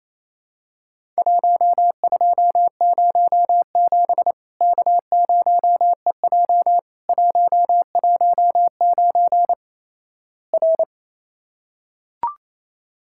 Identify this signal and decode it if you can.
Morse code, heard as a keyed tone, about 28 wpm, 700 Hz. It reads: 1207 K0EJ 119 R
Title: Worked All Europe DX Contest - QTC Exchange - 28wpm